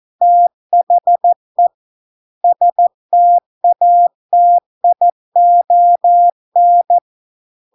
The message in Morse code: THE STATION

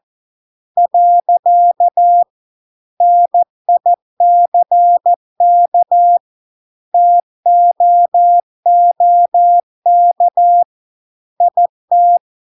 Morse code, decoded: . NICK TOOK IT

